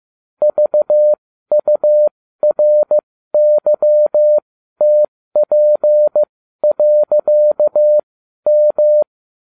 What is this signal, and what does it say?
Morse code, heard as a keyed tone, about 15 wpm, spaced as for 12 wpm, 600 Hz. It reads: VURYTP.M